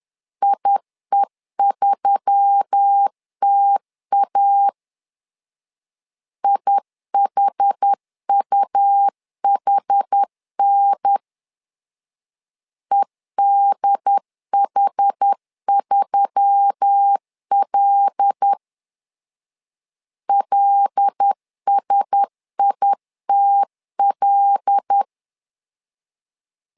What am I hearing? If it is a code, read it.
Morse code, heard as a keyed tone, about 10 wpm, 800 Hz. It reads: IE3TA IHUHN EDH3L LSITL
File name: LEZIONI / 5 LBVC / 5 L 10wpm 2259